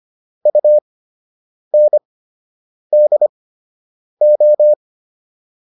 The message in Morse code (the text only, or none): UNDO